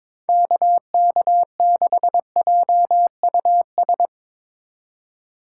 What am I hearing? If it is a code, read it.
Morse code, heard as a keyed tone, about 22 wpm, 700 Hz. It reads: KK6JUS
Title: USA Call Signs - 6 Characters - 4x Character Spacing - 22wpm